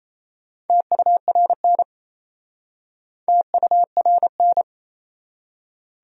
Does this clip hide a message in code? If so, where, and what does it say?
Yes: Morse code from 0.7 to 4.6 s, heard as a keyed tone, about 38 wpm, 700 Hz. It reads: TURN TURN